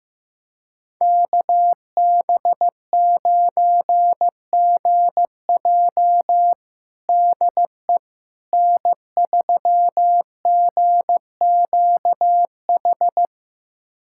Morse code, heard as a keyed tone, about 15 wpm, 700 Hz. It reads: KB9GJ DE N3GQH
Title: USA Callsign DE Callsign - 5 Characters - 15wpm